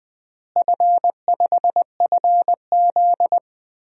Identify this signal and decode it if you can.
Morse code, heard as a keyed tone, about 20 wpm, 700 Hz. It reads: F5FZ
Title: DXCC-100 Call Signs - 4 Characters - Repeated 2x - 20wpm